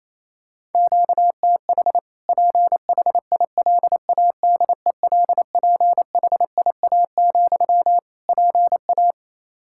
Morse code, heard as a keyed tone, about 28 wpm, 700 Hz. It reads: QTH PHILADELPHIA, PA
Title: Practice QSO - Exchange 1 - Beginning - 28wpm